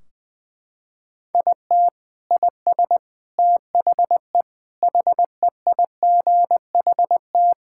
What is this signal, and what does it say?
Morse code, heard as a keyed tone, about 20 wpm, 700 Hz. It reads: IT IS THE HEIGHT
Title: Sentences from Top 1600 Words 20wpm